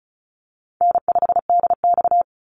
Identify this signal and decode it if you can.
Morse code, heard as a keyed tone, about 34 wpm, 700 Hz. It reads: N5DX
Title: Easy Call Signs Repeated 2x - 35wpm